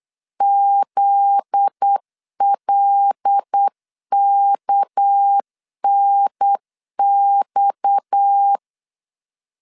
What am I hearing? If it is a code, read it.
Morse code, heard as a keyed tone, about 8 wpm, 800 Hz. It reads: ZLKNX
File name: LEZIONI / 7 GF9Z / 7 GF9Z 8wpm 3213